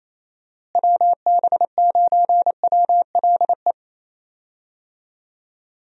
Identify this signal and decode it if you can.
Morse code, heard as a keyed tone, about 27 wpm, 700 Hz. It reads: WB9WLE
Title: USA Call Signs - 6 Characters - Repeated 3x - 28wpm